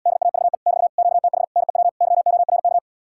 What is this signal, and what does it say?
Morse code, heard as a keyed tone, about 77 wpm, 700 Hz. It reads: DAVE <BT> QTH NEW YORK